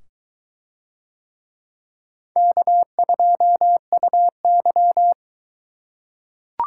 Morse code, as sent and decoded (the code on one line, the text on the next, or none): -.- ..--- ..- -.--
K2UY